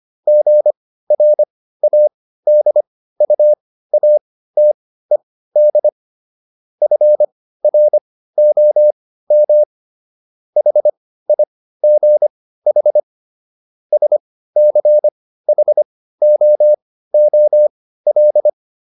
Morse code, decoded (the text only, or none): GRADUATED FROM HIGH SCHOOL